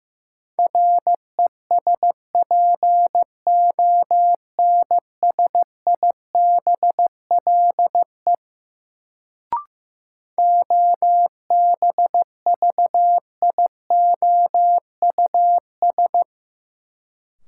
Morse code, heard as a keyed tone, about 15 wpm, 700 Hz. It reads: RESPONSIBLE OBVIOUS
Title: Sentences from Top 1600 Words - New Word Drill Alt - 15wpm